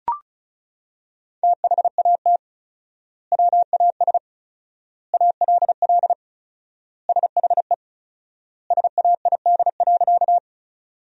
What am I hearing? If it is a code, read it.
Morse code, heard as a keyed tone, about 36 wpm, 700 Hz. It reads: THAT WAS ALL SHE SAID.
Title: 4x Word-Spacing - Sentences from Top 100 Words - 35wpm